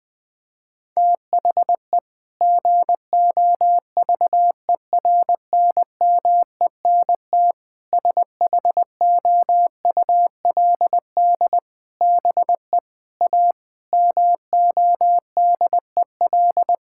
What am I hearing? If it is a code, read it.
Morse code, heard as a keyed tone, about 20 wpm, 700 Hz. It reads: THE GOVERNMENT SHOULD BE A MODEL
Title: Sentences from Top 1600 Words 20wpm